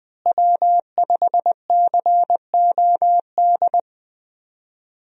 Morse code, heard as a keyed tone, about 20 wpm, 700 Hz. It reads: W5COD
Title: USA Call Signs - 5 Characters - Repeated 3x - 20wpm